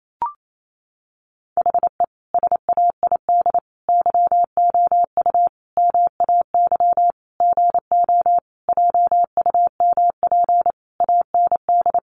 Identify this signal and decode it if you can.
Morse code, heard as a keyed tone, about 28 wpm, 700 Hz. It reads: HE SAID YOU MAY GO JUMP AND
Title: Sentences from Top 1300 Words 28wpm